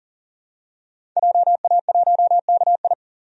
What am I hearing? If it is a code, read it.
Morse code, heard as a keyed tone, about 38 wpm, 700 Hz. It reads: JA1KI
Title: DXCC-100 Call Signs - 5 Characters - 40wpm